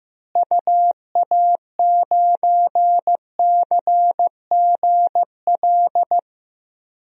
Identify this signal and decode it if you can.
Morse code, heard as a keyed tone, about 15 wpm, 700 Hz. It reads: UA9CGL
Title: DXCC-100 Call Signs - 6 Characters - Repeated 3x - 15wpm